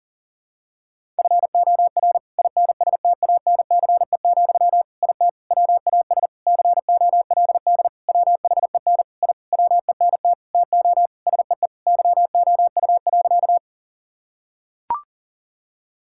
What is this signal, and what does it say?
Morse code, heard as a keyed tone, about 40 wpm, 700 Hz. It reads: FOR INSTANCE, IT WAS COLD WHEN I WENT TO SEE YOU.
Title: Sentences from Top 1000 Words - Review Part 5 - 40wpm